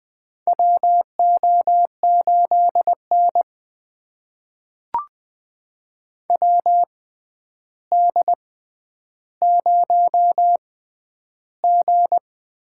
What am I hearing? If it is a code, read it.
Morse code, heard as a keyed tone, about 20 wpm, spaced as for 10 wpm, 700 Hz. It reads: WO8N WD0G
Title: USA Call Signs - 6x Character Spacing - 20wpm